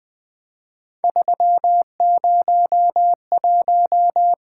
Morse code, heard as a keyed tone, about 20 wpm, 700 Hz. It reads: 301
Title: Three Letter-Number Combinations - 20wpm